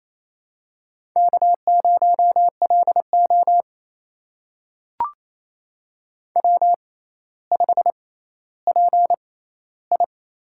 Morse code, heard as a keyed tone, about 27 wpm, spaced as for 14 wpm, 700 Hz. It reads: K0LO W5PI
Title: USA Call Signs - 4 Characters - 6x Character Spacing - 28wpm